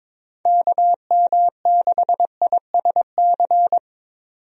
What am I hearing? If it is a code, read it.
Morse code, heard as a keyed tone, about 21 wpm, 700 Hz. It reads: KM6ISC
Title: USA Call Signs - 6 Characters - Repeated 3x - 22wpm